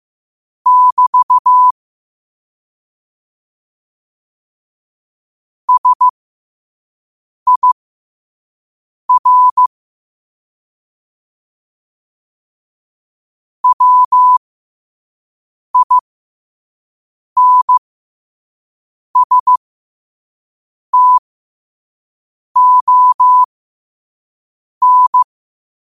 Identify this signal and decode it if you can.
Morse code, heard as a keyed tone, about 15 wpm, spaced as for 5 wpm, 1000 Hz. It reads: <BT> SIR WINSTON
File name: qotdfiles / 5wpm